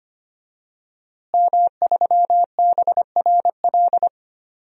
Morse code, heard as a keyed tone, about 25 wpm, 700 Hz. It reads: M3BRL
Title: DXCC-100 Call Signs - 5 Characters - Repeated 3x - 25wpm